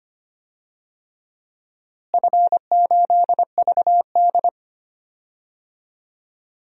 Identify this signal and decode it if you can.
Morse code, heard as a keyed tone, about 25 wpm, 700 Hz. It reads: F8VD
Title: DXCC-100 Call Signs - 4 Characters - Repeated 3x - 25wpm